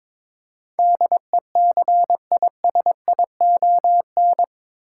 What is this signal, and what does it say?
Morse code, heard as a keyed tone, about 21 wpm, 700 Hz. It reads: DECISION
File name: practice / 22wpm / Eight Letter Words - 22wpm